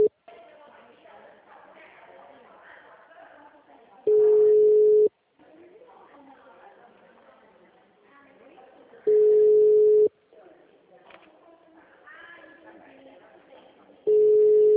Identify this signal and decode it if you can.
Ringback tone (Europe, 425 Hz), with speech.
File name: out- -2007- - - .40338